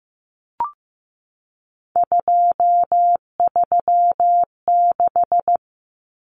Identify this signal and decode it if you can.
Morse code, heard as a keyed tone, about 15 wpm, 700 Hz. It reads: 236
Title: Numbers - 3-Digit 15wpm